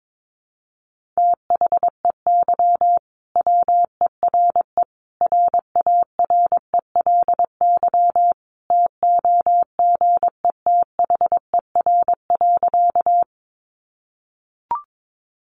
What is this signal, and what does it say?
Morse code, heard as a keyed tone, about 22 wpm, 700 Hz. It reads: THEY WERE RARELY TOGETHER.
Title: Sentences from Top 1400 Words 22wpm